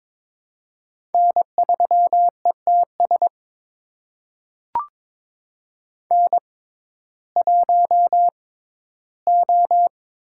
Morse code, heard as a keyed tone, about 21 wpm, spaced as for 13 wpm, 700 Hz. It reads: N3ETS N1O
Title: USA Call Signs - 5 Characters - 6x Character Spacing - 22wpm